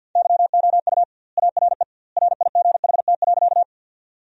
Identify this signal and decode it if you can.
Morse code, heard as a keyed tone, about 50 wpm, 700 Hz. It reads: YOU ARE RIGHT.
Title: 2x Word-Spacing - Sentences from Top 200 Words - 50wpm